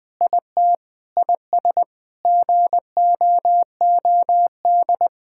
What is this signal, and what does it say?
Morse code, heard as a keyed tone, about 21 wpm, 700 Hz. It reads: IT IS GOOD
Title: Sentences with 4-Word Phrases - 20wpm